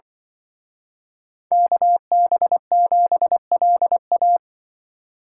Morse code, heard as a keyed tone, about 23 wpm, 700 Hz. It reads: KB7LA
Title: Speed Racing - USA 2x2 Call Signs 20wpm